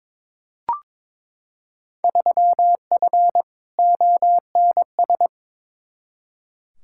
Morse code, heard as a keyed tone, about 22 wpm, 700 Hz. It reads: 3F ONS
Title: Field Day Exchanges - 22wpm